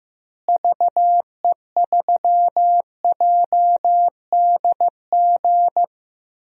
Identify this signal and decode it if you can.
Morse code, heard as a keyed tone, about 15 wpm, 700 Hz. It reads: VE3JDG
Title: DXCC-100 Call Signs - 6 Characters - Repeated 3x - 15wpm